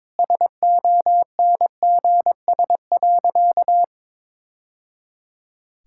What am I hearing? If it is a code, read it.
Morse code, heard as a keyed tone, about 22 wpm, 700 Hz. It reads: SONGS.
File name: practice / 22wpm / Easy Sentences with Hints - 22wpm